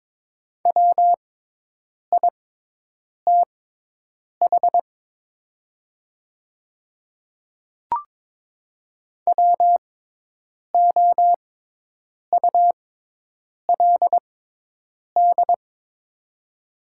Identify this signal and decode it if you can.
Morse code, heard as a keyed tone, about 22 wpm, spaced as for 8 wpm, 700 Hz. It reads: WITH WOULD
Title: 6x Character Spacing - Top 300 Words - 22wpm